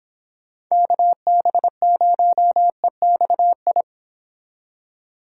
Morse code, heard as a keyed tone, about 26 wpm, 700 Hz. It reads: KB0EXI